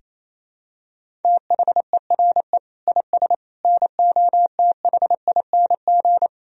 Morse code, heard as a keyed tone, about 28 wpm, 700 Hz. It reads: THERE IS NOTHING
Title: Sentences from Top 1000 Words - Review Part 3 - 28wpm